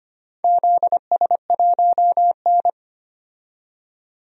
Morse code, heard as a keyed tone, about 25 wpm, 700 Hz. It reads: ZS1N
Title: DXCC-100 Call Signs - 4 Characters - 25wpm